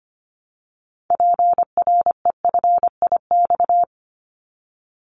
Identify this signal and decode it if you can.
Morse code, heard as a keyed tone, about 25 wpm, 700 Hz. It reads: PREFIX
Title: Callsign Prefixes for Top 25 DXCC - Repeated 3x - 25wpm